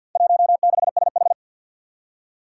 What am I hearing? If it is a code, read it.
Morse code, heard as a keyed tone, about 50 wpm, 700 Hz. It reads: 16SH
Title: Four Challenging Letter-Number Combinations 50wpm